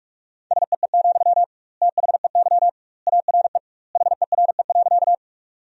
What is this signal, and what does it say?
Morse code, heard as a keyed tone, about 46 wpm, 700 Hz. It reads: SEE, THEY ARE HERE.
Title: 2x Word-Spacing - Sentences from Top 200 Words - 45wpm